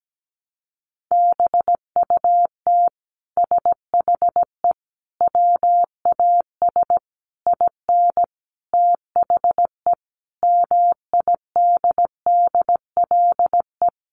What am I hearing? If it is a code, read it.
Morse code, heard as a keyed tone, about 17 wpm, 700 Hz. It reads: BUT SHE WAS IN THE MIDDLE
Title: Sentences from Top 1200 Words 17wpm